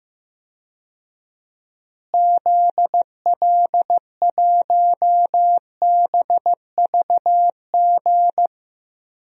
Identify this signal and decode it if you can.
Morse code, heard as a keyed tone, about 15 wpm, 700 Hz. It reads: ZL1BVG
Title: DXCC-100 Call Signs - 6 Characters - Repeated 3x - 15wpm